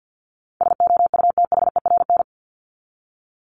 Morse code, heard as a keyed tone, about 50 wpm, 700 Hz. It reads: SOUTHERN
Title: Top 1751-2000 Words 50wpm